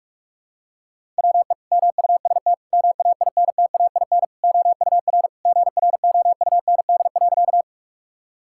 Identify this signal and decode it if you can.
Morse code, heard as a keyed tone, about 41 wpm, 700 Hz. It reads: WE MUST MAINTAIN OUR GROUND.
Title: Sentences from Top 1000 Words - Review Part 5 - 45wpm